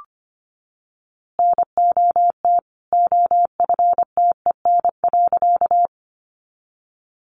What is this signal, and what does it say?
Morse code, heard as a keyed tone, about 25 wpm, 700 Hz. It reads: NOT OFTEN.
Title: A Sound Of Thunder 25wpm